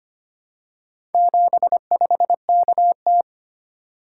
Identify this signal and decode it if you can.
Morse code, heard as a keyed tone, about 25 wpm, 700 Hz. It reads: 75KT
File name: practice / 25wpm / Four Letter-Number Combinations - 25wpm